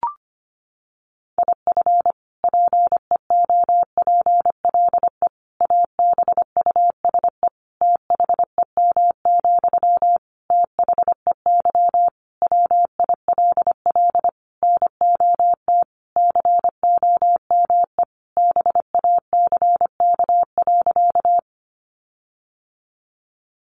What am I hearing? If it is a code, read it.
Morse code, heard as a keyed tone, about 25 wpm, 700 Hz. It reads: IF PEOPLE ABUSE THEM, THEY WILL NOT COME BACK.
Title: Sentences from Top 1300 Words 25wpm